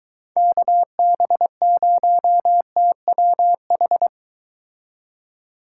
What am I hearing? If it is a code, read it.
Morse code, heard as a keyed tone, about 23 wpm, 700 Hz. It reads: KB0TWH